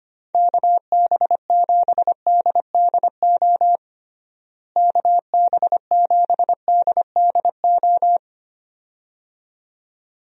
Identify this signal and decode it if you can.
Morse code, heard as a keyed tone, about 25 wpm, 700 Hz. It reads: KB7DDO KB7DDO
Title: USA Call Signs - 6 Characters - Repeated 2x - 25wpm